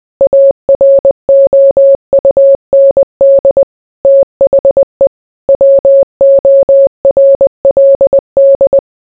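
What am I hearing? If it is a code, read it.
Morse code, heard as a keyed tone, about 20 wpm, 550 Hz. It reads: AROUND THE WORLD
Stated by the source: Morse code mixed with noise and music as background sound for sleep.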